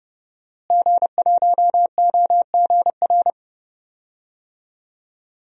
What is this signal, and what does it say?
Morse code, heard as a keyed tone, about 30 wpm, 700 Hz. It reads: G1OGR
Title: DXCC-100 Call Signs - 5 Characters - Repeated 2x - 30wpm